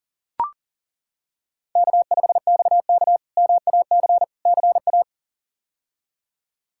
Morse code, heard as a keyed tone, about 40 wpm, 700 Hz. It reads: K5XK MAC CA
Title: K1USN SST QSO Practice - 40wpm